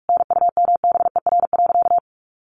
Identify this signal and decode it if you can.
Morse code, heard as a keyed tone, about 44 wpm, 700 Hz. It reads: NUMBER.
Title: Sentences from Top 1400 Words 45wpm